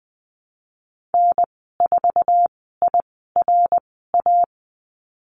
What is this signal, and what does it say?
Morse code, heard as a keyed tone, about 20 wpm, spaced as for 15 wpm, 700 Hz. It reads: N4IRA
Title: 055 Lesson 10g - Letter r - Call Signs 20wpm